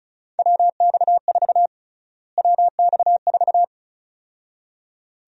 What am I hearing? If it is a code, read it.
Morse code, heard as a keyed tone, about 36 wpm, 700 Hz. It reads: WX4 WX4
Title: USA Call Sign Prefixes - Repeated 2x - 35wpm